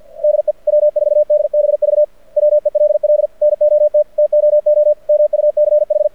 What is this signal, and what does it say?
Morse code, heard as a keyed tone, about 52 wpm, 600 Hz. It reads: DE W4DBV WELL NOT TOO MUCH
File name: archived / CRSnet